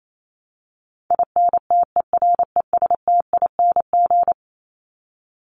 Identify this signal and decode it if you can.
Morse code, heard as a keyed tone, about 28 wpm, 700 Hz. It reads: INTERESTING
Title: ING Suffix - Top 100 Words with Base Words 28wpm